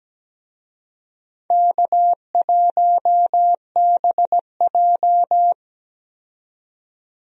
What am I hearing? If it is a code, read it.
Morse code, heard as a keyed tone, about 17 wpm, 700 Hz. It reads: K1BJ
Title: USA Call Signs - 4 Characters - 17wpm